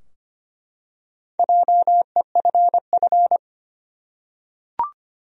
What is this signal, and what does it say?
Morse code, heard as a keyed tone, about 25 wpm, 700 Hz. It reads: JEFF
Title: USA Common Names - Max 5 Letters - 25wpm